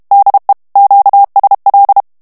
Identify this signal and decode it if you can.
Morse code, heard as a keyed tone, about 32 wpm, 800 Hz. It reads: DE QSL
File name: morse mp3 / ESPAGNOL TEXTE EN CLAIR 1 VITESSE 30 MOTS MINUTES